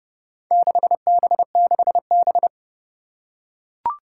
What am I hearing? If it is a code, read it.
Morse code, heard as a keyed tone, about 28 wpm, 700 Hz. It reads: 6B6B
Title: Four Challenging Letter-Number Combinations 30wpm